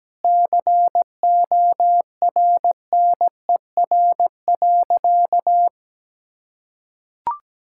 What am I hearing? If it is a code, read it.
Morse code, heard as a keyed tone, about 17 wpm, 700 Hz. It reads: CORNER.
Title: Sentences from Top 1600 Words 17wpm